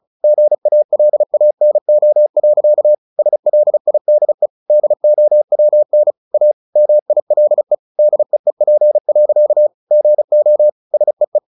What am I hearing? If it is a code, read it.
Morse code, heard as a keyed tone, about 34 wpm, 600 Hz. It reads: GALANO. SLIDE DOWN A MILE DEEP. GO SEE
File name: the old man and the sea 2 35wpm